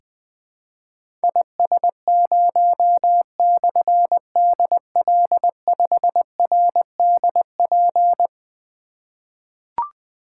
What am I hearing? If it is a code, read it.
Morse code, heard as a keyed tone, about 20 wpm, 700 Hz. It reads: IS0/DL5RDP
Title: Difficult Call Signs 20wpm